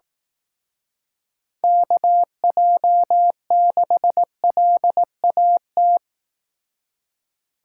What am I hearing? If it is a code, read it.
Morse code, heard as a keyed tone, about 18 wpm, 700 Hz. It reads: KJ6LAT
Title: Speed Racing - USA 2x3 Call Signs 15wpm